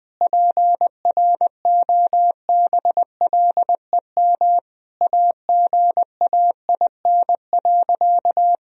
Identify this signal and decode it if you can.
Morse code, heard as a keyed tone, about 20 wpm, 700 Hz. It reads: PROBLEM AGAIN.
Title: Sentences from Top 1000 Words - Review Part 3 - 20wpm